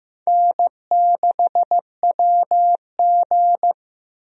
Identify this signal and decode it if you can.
Morse code, heard as a keyed tone, about 15 wpm, 700 Hz. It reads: N6WG
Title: USA Call Signs - 4 Characters - 15wpm